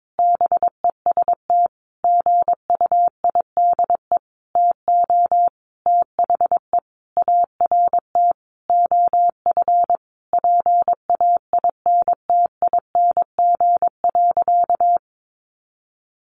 Morse code, heard as a keyed tone, about 22 wpm, 700 Hz. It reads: BEST GUIDE TO THE ART OF PAINTING.
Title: Sentences from Top 1200 Words 22wpm